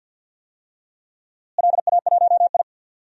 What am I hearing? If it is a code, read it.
Morse code, heard as a keyed tone, about 47 wpm, 700 Hz. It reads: LA1I
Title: DXCC-100 Call Signs - 4 Characters - Repeated 2x - 50wpm